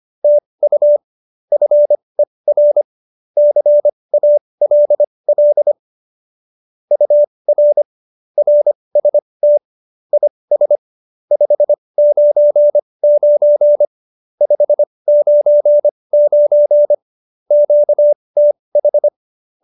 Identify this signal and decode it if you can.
Morse code, heard as a keyed tone, about 25 wpm, spaced as for 20 wpm, 590 Hz. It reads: TU FER CALL UR RST IS 599 599 QTH
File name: qso216b 20